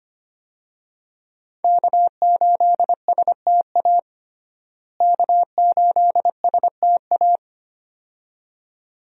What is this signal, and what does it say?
Morse code, heard as a keyed tone, about 25 wpm, 700 Hz. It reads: K8STA K8STA
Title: USA Call Signs - 5 Characters - Repeated 2x - 25wpm